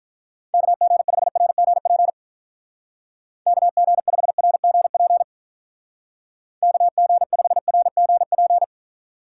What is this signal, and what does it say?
Morse code, heard as a keyed tone, about 44 wpm, 700 Hz. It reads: KG5RGP KG5RGP KG5RGP
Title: Speed Racing - USA 2x3 Call Signs 35wpm